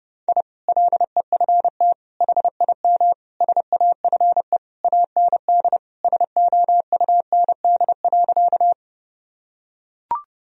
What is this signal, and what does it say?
Morse code, heard as a keyed tone, about 32 wpm, 700 Hz. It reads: I LEFT HIM SAFE AND SOUND.
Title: Sentences from Top 1000 Words - Review Part 1 - 30wpm